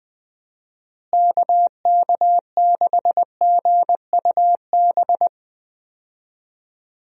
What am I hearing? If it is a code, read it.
Morse code, heard as a keyed tone, about 21 wpm, 700 Hz. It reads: KK6GUB
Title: USA Call Signs - 6 Characters - Repeated 3x - 20wpm